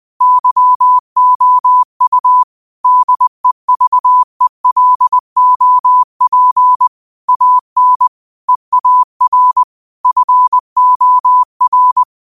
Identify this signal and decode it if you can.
Morse code, heard as a keyed tone, about 20 wpm, 1000 Hz. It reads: YOU DEVELOP AN EAR FOR
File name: qotdfiles / 20wpm